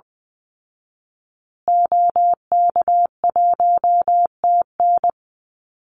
Morse code, heard as a keyed tone, about 20 wpm, 700 Hz. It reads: OK1TN
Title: Call Signs Repeated 2x - 20wpm